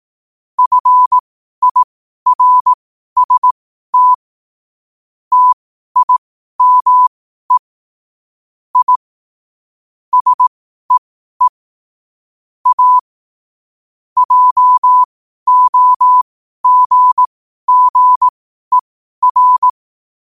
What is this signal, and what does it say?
Morse code, heard as a keyed tone, about 18 wpm, spaced as for 12 wpm, 1000 Hz. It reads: FIRST TIME I SEE A JOGGER
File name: qotdfiles / 12wpm